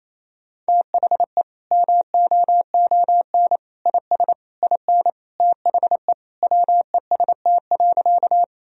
Morse code, heard as a keyed tone, about 28 wpm, 700 Hz. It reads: THE MOON IS IN THE WEST.
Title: Sentences from Top 1000 Words - Review Part 3 - 28wpm